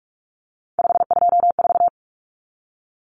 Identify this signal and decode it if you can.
Morse code, heard as a keyed tone, about 44 wpm, 700 Hz. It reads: 524
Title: Common Cut Numbers - 3-Digit 45wpm